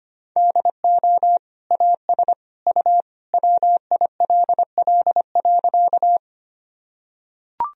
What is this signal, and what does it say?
Morse code, heard as a keyed tone, about 25 wpm, 700 Hz. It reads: DO AS U WILL.
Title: Sentences from Top 100 Words and Abbreviations 25wpm